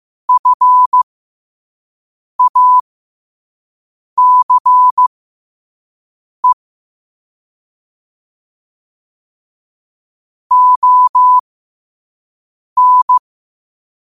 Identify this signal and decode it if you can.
Morse code, heard as a keyed tone, about 15 wpm, spaced as for 5 wpm, 1000 Hz. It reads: FACE ON
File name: qotdfiles / 5wpm